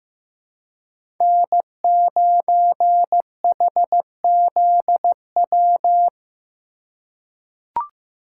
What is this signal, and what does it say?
Morse code, heard as a keyed tone, about 15 wpm, 700 Hz. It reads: N9HZW
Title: USA Call Signs - 5 Characters - 2x Character Spacing - 15wpm